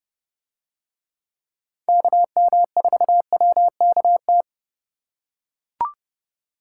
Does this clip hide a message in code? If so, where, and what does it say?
Yes: Morse code from 1.9 to 4.4 s, heard as a keyed tone, about 29 wpm, 700 Hz. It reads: KM4WKT